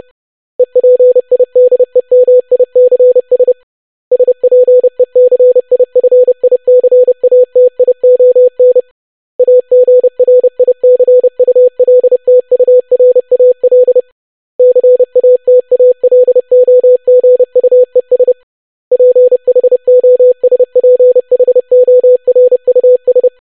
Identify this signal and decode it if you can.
Morse code, heard as a keyed tone, about 30 wpm, 500 Hz. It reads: EPIDEMICS SPECIFICATION AGRICULTURAL CATALOGUES PHOSPHORUS
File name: longwords-25wpm